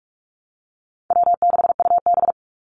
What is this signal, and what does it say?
Morse code, heard as a keyed tone, about 44 wpm, 700 Hz. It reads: W6UB
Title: NA SOTA Activator Callsigns 45wpm